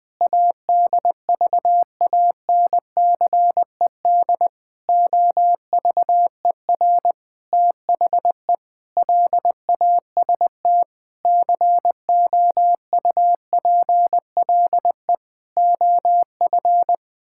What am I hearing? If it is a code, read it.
Morse code, heard as a keyed tone, about 20 wpm, 700 Hz. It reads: ADVANCED OVER THE LAST COUPLE OF
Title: Sentences with 6-Word Phrases - 20wpm